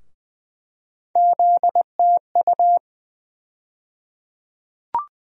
Morse code, heard as a keyed tone, about 20 wpm, 700 Hz. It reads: ZTU
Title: USA Call Sign Postfixes - 3 Letters 20wpm